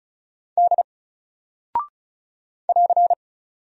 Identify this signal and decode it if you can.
Morse code, heard as a keyed tone, about 36 wpm, 700 Hz. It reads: D <AR>
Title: Single Character (Letter:Number:Punctuation:Prosign) Alt 35wpm